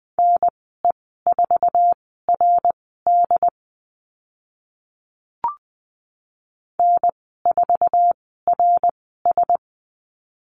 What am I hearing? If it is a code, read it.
Morse code, heard as a keyed tone, about 20 wpm, spaced as for 14 wpm, 700 Hz. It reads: NE4RD N4RS
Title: 069 Lesson 12g - Letter d - Call Signs 20wpm